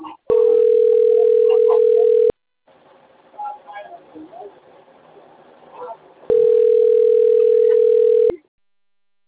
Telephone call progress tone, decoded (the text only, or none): ringback tone